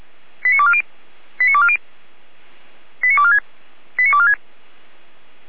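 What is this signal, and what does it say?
Five-tone selective call (ZVEI), heard as quick runs of five tones, tones of about 70 ms. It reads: 89240 89240 89246 89246
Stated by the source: ZVEI five-tone selective call received on an SDR.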